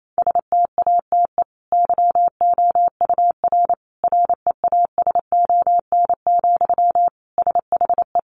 Morse code, heard as a keyed tone, about 28 wpm, 700 Hz. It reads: STATE YOUR REASON, SHE
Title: Sentences from Top 700 Words 28wpm